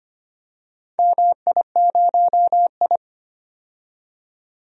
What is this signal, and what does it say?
Morse code, heard as a keyed tone, about 25 wpm, 700 Hz. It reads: MI0I
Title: DXCC-100 Call Signs - 4 Characters - 25wpm